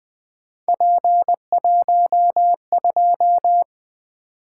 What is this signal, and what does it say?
Morse code, heard as a keyed tone, about 20 wpm, 700 Hz. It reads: P12
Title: Three Letter-Number Combinations - 20wpm